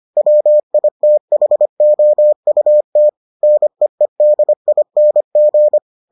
Morse code, heard as a keyed tone, about 25 wpm, 600 Hz. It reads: WITHOUT NEEDING